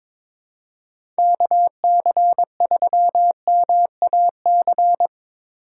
Morse code, heard as a keyed tone, about 22 wpm, 700 Hz. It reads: KC3MAC
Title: USA Call Signs - 6 Characters - 2x Character Spacing - 22wpm